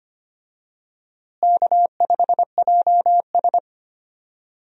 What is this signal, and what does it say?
Morse code, heard as a keyed tone, about 25 wpm, 700 Hz. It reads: K5JS
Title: USA Call Signs - 4 Characters - Repeated 3x - 25wpm